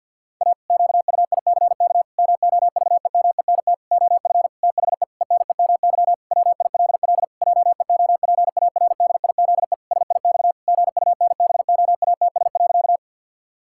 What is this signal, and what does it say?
Morse code, heard as a keyed tone, about 50 wpm, 700 Hz. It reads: A QUICK MOVEMENT OF THE ENEMY WILL JEOPARDIZE SIX GUNBOATS.
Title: Pangram Challenge v2 - 50wpm